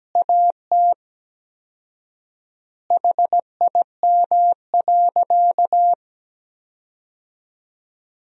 Morse code, heard as a keyed tone, about 17 wpm, 700 Hz. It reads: AT HIM.
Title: 4x Word-Spacing - Sentences from Top 100 Words - 17wpm